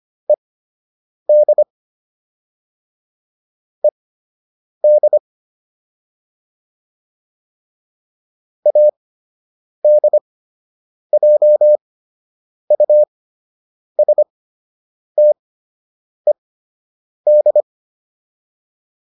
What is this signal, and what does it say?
Morse code, heard as a keyed tone, about 25 wpm, spaced as for 8 wpm, 600 Hz. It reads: ED ED ADJUSTED